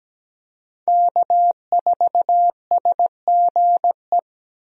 Morse code, heard as a keyed tone, about 17 wpm, 700 Hz. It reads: K4SGE
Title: USA Call Signs - 5 Characters - Repeated 3x - 17wpm